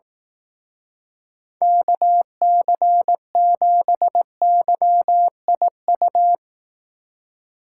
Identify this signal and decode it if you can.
Morse code, heard as a keyed tone, about 18 wpm, 700 Hz. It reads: KC7YIU
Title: Speed Racing - USA 2x3 Call Signs 15wpm